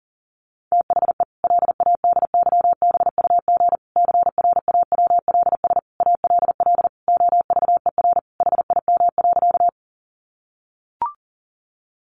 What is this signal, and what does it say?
Morse code, heard as a keyed tone, about 40 wpm, 700 Hz. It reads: THE LADYBUG CRAWLS ALL OVER HIM.
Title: Additional Easy Phrases 40wpm